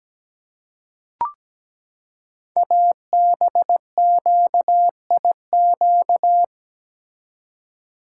Morse code, heard as a keyed tone, about 17 wpm, 700 Hz. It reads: ABQIQ
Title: Code Groups - 5 Letters - 17wpm